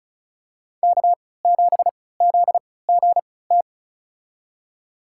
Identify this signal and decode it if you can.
Morse code, heard as a keyed tone, about 36 wpm, spaced as for 20 wpm, 700 Hz. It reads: K7ZGT
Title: USA Call Signs - 5 Characters - 3x Character Spacing - 35wpm